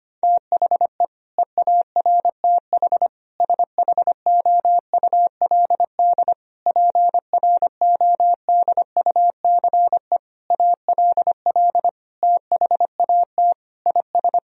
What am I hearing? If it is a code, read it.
Morse code, heard as a keyed tone, about 25 wpm, 700 Hz. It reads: THE EARTH SHOULD PRODUCE ALL THAT IS
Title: Sentences from Top 1000 Words - Review Part 3 - 25wpm